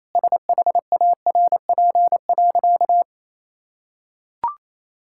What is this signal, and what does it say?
Morse code, heard as a keyed tone, about 28 wpm, 700 Hz. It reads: SHARP.
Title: Sentences from Top 1600 Words 28wpm